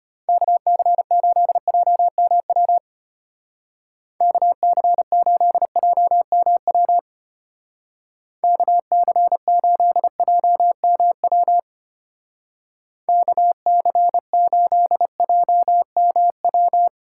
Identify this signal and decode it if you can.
Morse code, heard as a keyed tone, about 33 wpm, 700 Hz. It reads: KC8JMW KC8JMW KC8JMW KC8JMW